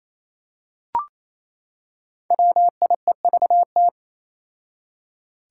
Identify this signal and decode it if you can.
Morse code, heard as a keyed tone, about 28 wpm, 700 Hz. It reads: WIEVT
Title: Five Challenging Letter-Number Combinations 28wpm